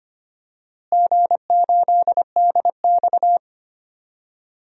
Morse code, heard as a keyed tone, about 25 wpm, 700 Hz. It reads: G8DX
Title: DXCC-100 Call Signs - 4 Characters - Repeated 2x - 25wpm